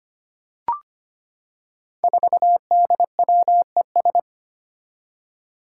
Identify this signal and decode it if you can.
Morse code, heard as a keyed tone, about 25 wpm, 700 Hz. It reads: 4DWES
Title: Five Challenging Letter-Number Combinations 25wpm